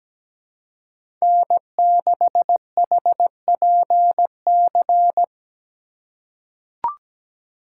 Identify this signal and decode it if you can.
Morse code, heard as a keyed tone, about 17 wpm, 700 Hz. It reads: N6HPC
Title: USA Call Signs - 5 Characters - 6x Character Spacing - 17wpm